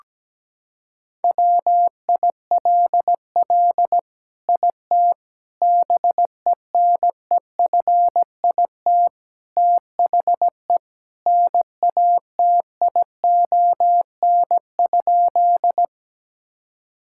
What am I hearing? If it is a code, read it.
Morse code, heard as a keyed tone, about 17 wpm, 700 Hz. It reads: WILL IT BENEFIT THE NATION?
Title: Sentences from Top 1000 Words - Review Part 6 - 17wpm